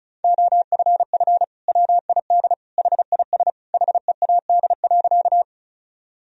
Morse code, heard as a keyed tone, about 36 wpm, 700 Hz. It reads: OFF WID HIS HEAD.
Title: Sentences from Top 300 Words and Abbreviations 35wpm